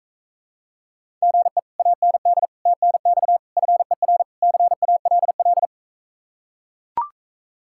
Morse code, heard as a keyed tone, about 43 wpm, 700 Hz. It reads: GE AND TNX FER CALL